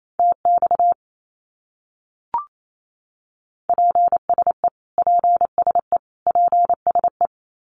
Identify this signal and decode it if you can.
Morse code, heard as a keyed tone, about 28 wpm, 700 Hz. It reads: TX PSE PSE PSE
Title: Abbreviations Repeated 3x - 28wpm